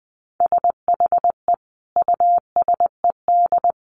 Morse code, heard as a keyed tone, about 20 wpm, 700 Hz. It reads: SHE USED
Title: Sentences from Top 1400 Words 20wpm